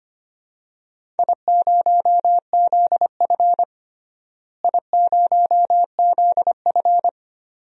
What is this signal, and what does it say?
Morse code, heard as a keyed tone, about 25 wpm, 700 Hz. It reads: I0ZF I0ZF
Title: DXCC-100 Call Signs - 4 Characters - Repeated 2x - 25wpm